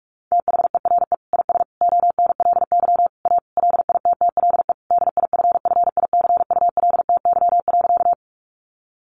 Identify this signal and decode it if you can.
Morse code, heard as a keyed tone, about 44 wpm, 700 Hz. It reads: THERE IS ONLY A LITTLE DIFFICULTY.
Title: Sentences from Top 1500 Words 45wpm